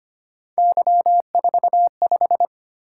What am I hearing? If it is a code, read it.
Morse code, heard as a keyed tone, about 25 wpm, 700 Hz. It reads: Y45
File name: Three Letter-Number Combinations - 25wpm